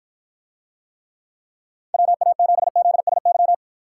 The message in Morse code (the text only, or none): WA7ZSY